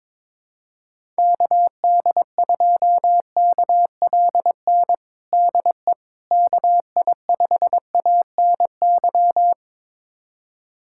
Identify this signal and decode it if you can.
Morse code, heard as a keyed tone, about 22 wpm, 700 Hz. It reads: KD2KLN DE KI5ANY
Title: USA Callsign DE Callsign - 6 Characters - 22wpm